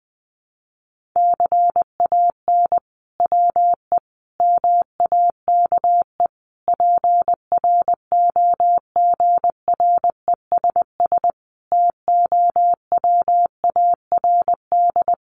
Morse code, heard as a keyed tone, about 20 wpm, 700 Hz. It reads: CAN WE MAKE PROGRESS TOWARD
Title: Sentences from Top 1500 Words 20wpm